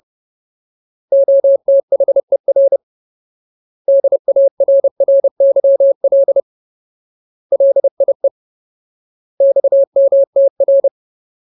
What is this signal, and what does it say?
Morse code, heard as a keyed tone, about 30 wpm, 550 Hz. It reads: OTHER DARRYL LIE XMTR
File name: Ultimate Word List / 30WPM Ultimate